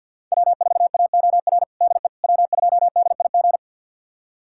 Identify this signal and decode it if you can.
Morse code, heard as a keyed tone, about 52 wpm, 700 Hz. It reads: W4AOR DE W2DIG
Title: USA Callsign DE Callsign - 5 Characters - 50wpm